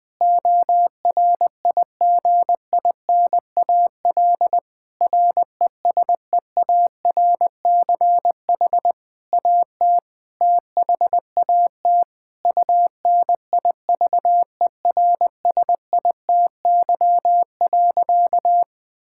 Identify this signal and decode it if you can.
Morse code, heard as a keyed tone, about 21 wpm, 700 Hz. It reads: ORIGINAL RESEARCH AT THAT UNIVERSITY.
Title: Sentences from Top 1000 Words - Review Part 6 - 20wpm